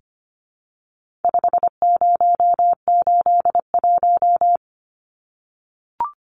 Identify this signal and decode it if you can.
Morse code, heard as a keyed tone, about 25 wpm, 700 Hz. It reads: 5081
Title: Numbers - 4-Digit 25wpm